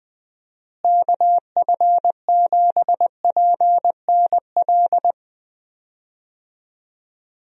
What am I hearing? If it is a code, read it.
Morse code, heard as a keyed tone, about 21 wpm, 700 Hz. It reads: KF7PNL
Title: USA Call Signs - 6 Characters - Repeated 2x - 20wpm